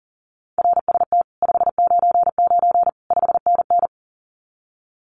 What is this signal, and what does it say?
Morse code, heard as a keyed tone, about 42 wpm, 700 Hz. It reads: RST 599 5NN
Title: Common QSO Phrases Encore 40wpm